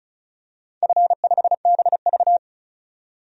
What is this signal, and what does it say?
Morse code, heard as a keyed tone, about 36 wpm, 700 Hz. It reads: F5BV
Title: DXCC-100 Call Signs - 4 Characters - 35wpm